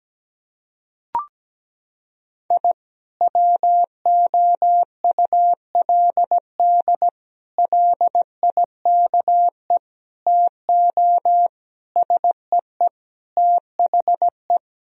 Morse code, heard as a keyed tone, about 17 wpm, 700 Hz. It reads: I WOULD LIKE TO SEE THE
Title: Sentences from Top 1000 Words - Review Part 4 - 17wpm